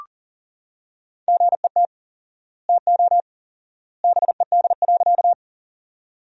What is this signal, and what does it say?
Morse code, heard as a keyed tone, about 40 wpm, 700 Hz. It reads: GET TO BED.
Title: 4x Word-Spacing - Sentences from Top 500 Words - 40wpm